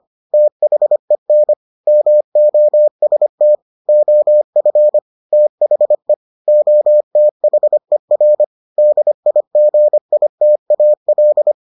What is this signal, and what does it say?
Morse code, heard as a keyed tone, about 25 wpm, 600 Hz. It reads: THEN MOST OF THE OTHER DIGITAL